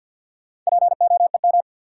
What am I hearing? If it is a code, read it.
Morse code, heard as a keyed tone, about 54 wpm, 700 Hz. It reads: POEM
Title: Four Letter Words - 50wpm